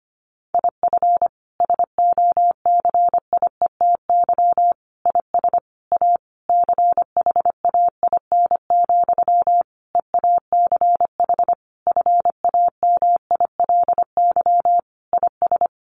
Morse code, heard as a keyed tone, about 25 wpm, 700 Hz. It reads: IF SOCIETY IS A CHAIN, EACH FAMILY IS
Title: Sentences from Top 1500 Words 25wpm